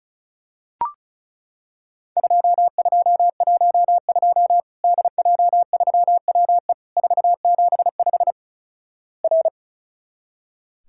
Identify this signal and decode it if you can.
Morse code, heard as a keyed tone, about 36 wpm, 700 Hz. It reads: 2212 DJ3WE 475 R
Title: Worked All Europe DX Contest - QTC Exchange - 35wpm